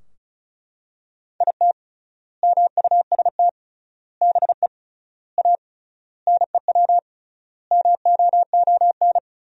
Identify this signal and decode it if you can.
Morse code, heard as a keyed tone, about 34 wpm, 700 Hz. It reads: IT MUST BE A NEW MOON
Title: 3x Word-Spacing - Sentences from Top 500 Words - 35wpm